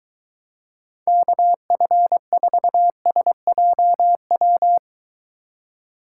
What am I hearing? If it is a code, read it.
Morse code, heard as a keyed tone, about 23 wpm, 700 Hz. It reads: KF4SJW